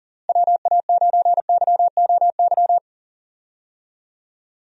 Morse code, heard as a keyed tone, about 40 wpm, 700 Hz. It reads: WA9YOY
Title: USA Call Signs - 6 Characters - Repeated 2x - 40wpm